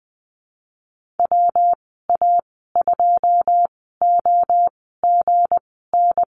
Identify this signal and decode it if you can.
Morse code, heard as a keyed tone, about 20 wpm, spaced as for 14 wpm, 700 Hz. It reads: WA2OGN
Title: 213 Lesson 34g - Number 8 - Call Signs 20wpm